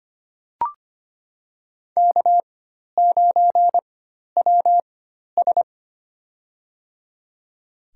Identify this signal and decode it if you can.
Morse code, heard as a keyed tone, about 25 wpm, spaced as for 12 wpm, 700 Hz. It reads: K9WS
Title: USA Call Signs - 4 Characters - 4x Character Spacing - 25wpm